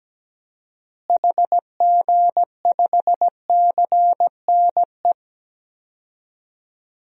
Morse code, heard as a keyed tone, about 17 wpm, 700 Hz. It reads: HG5CNE
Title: DXCC-100 Call Signs - 6 Characters - Repeated 2x - 17wpm